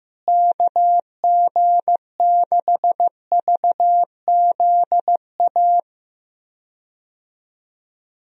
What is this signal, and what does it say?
Morse code, heard as a keyed tone, about 15 wpm, 700 Hz. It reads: KG6VZA
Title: USA Call Signs - 6 Characters - Repeated 3x - 15wpm